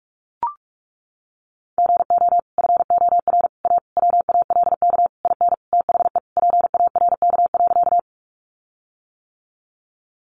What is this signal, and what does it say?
Morse code, heard as a keyed tone, about 44 wpm, 700 Hz. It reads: GO FOR A WALK IN THE PARK.
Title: Sentences from Top 1300 Words 45wpm